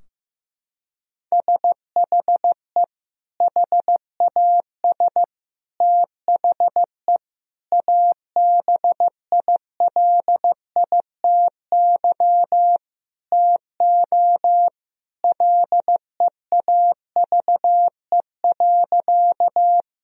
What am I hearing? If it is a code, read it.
Morse code, heard as a keyed tone, about 15 wpm, 700 Hz. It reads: SHE HAS THE ABILITY TO LEAVE.
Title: Sentences from Top 1000 Words - Review Part 3 - 15wpm